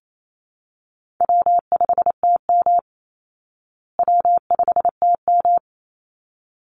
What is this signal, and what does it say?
Morse code, heard as a keyed tone, about 28 wpm, 700 Hz. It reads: W5TM W5TM
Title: Easy Call Signs Repeated 2x - 28wpm